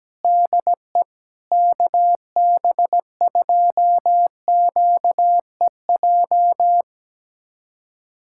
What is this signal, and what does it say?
Morse code, heard as a keyed tone, about 17 wpm, 700 Hz. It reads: DE KB2QEJ
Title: USA Callsign DE Callsign 17wpm